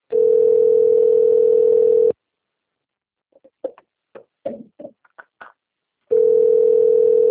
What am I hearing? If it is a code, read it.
Ringback tone (North America), 440 + 480 Hz.
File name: rg-1600-unknown- - - .8893